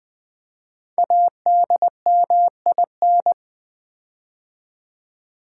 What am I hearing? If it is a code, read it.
Morse code, heard as a keyed tone, about 20 wpm, 700 Hz. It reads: ADMIN
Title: Build A Word - 5 Letters At A Time - 10-Letter Words - 20wpm